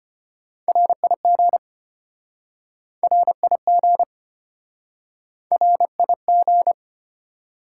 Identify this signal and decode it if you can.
Morse code, heard as a keyed tone, about 33 wpm, 690 Hz. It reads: RIG RIG RIG